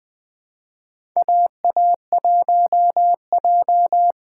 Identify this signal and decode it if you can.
Morse code, heard as a keyed tone, about 21 wpm, 700 Hz. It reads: AA1J
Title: USA Callsign DE Callsign - 4 Characters - 20wpm